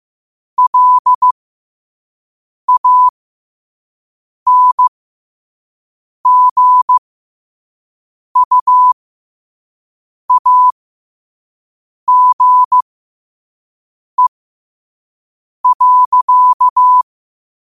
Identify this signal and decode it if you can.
Morse code, heard as a keyed tone, about 15 wpm, spaced as for 5 wpm, 1000 Hz. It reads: LANGUAGE.